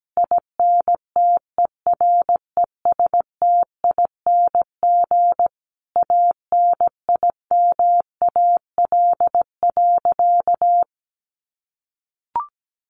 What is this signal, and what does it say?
Morse code, heard as a keyed tone, about 17 wpm, 700 Hz. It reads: INTERESTING ANIMAL.
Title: Easy Phrases 17wpm